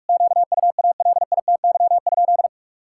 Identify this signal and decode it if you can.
Morse code, heard as a keyed tone, about 47 wpm, 700 Hz. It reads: QUALITY?
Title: Sentences from Top 1000 Words - Review Part 6 - 45wpm